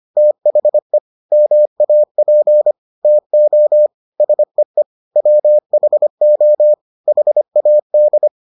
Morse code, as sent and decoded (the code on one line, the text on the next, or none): - .... / -- .- .--. / - --- / ... . / .-- .... --- / .... .- -..
THE MAP TO SEE WHO HAD